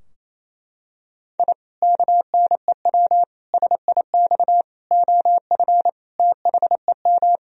Morse code, heard as a keyed tone, about 28 wpm, 700 Hz. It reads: I KNEW SIX OF THEM
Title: Sentences from Top 1000 Words - Review Part 2 - 28wpm